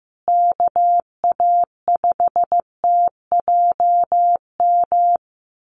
Morse code, heard as a keyed tone, about 15 wpm, 700 Hz. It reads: KA5TJM
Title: USA Call Signs Repeated 3x 15wpm